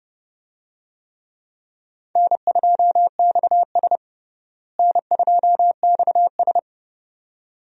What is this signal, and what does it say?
Morse code, heard as a keyed tone, about 32 wpm, 700 Hz. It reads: N2XS N2XS
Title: USA Call Signs - 4 Characters - Repeated 2x - 30wpm